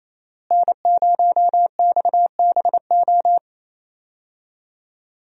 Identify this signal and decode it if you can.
Morse code, heard as a keyed tone, about 28 wpm, 700 Hz. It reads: N0XBO
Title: USA Call Signs - 5 Characters - Repeated 2x - 28wpm